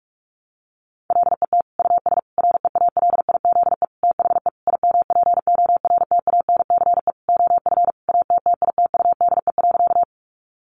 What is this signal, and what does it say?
Morse code, heard as a keyed tone, about 44 wpm, 700 Hz. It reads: LET US REALIZE THE IMPORTANCE OF ATTITUDE.
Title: Sentences from Top 1300 Words 45wpm